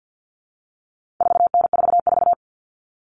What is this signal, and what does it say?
Morse code, heard as a keyed tone, about 50 wpm, 700 Hz. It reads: 4N44
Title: All Cut Numbers - 4-Digit 50wpm